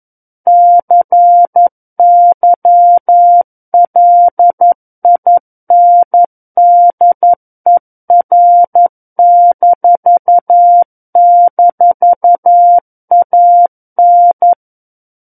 Morse code, heard as a keyed tone, about 11 wpm, 700 Hz. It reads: CYLINDER--AN
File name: Book One / 02-Chapter 2 - 11 WPM